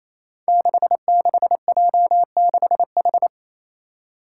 Morse code, heard as a keyed tone, about 28 wpm, 700 Hz. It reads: 66J6H
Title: Five Challenging Letter-Number Combinations 28wpm